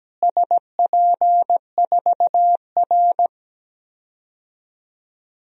Morse code, heard as a keyed tone, about 17 wpm, 700 Hz. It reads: SP4R
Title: DXCC-100 Call Signs - 4 Characters - Repeated 2x - 17wpm